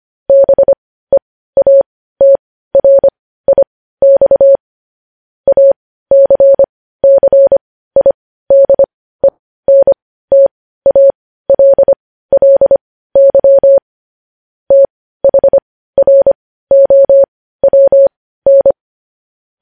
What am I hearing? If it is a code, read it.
Morse code, heard as a keyed tone, about 25 wpm, spaced as for 15 wpm, 550 Hz. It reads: BEATRIX ACCIDENTALLY THROWN
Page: News headlines at 15